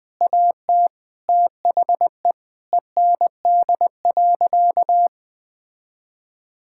Morse code, heard as a keyed tone, about 20 wpm, 700 Hz. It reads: AT THE END.
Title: Sentences with 3-Word Phrases - 20wpm